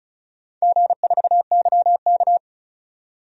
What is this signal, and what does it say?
Morse code, heard as a keyed tone, about 36 wpm, 700 Hz. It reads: G4YK
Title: DXCC-100 Call Signs - 4 Characters - 35wpm